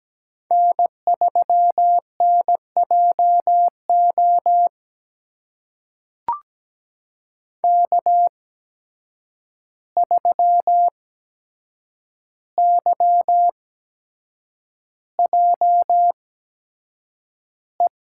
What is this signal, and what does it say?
Morse code, heard as a keyed tone, about 17 wpm, spaced as for 7 wpm, 700 Hz. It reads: N3NJO K3YJE
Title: USA Call Signs - 5 Characters - 8x Character Spacing - 17wpm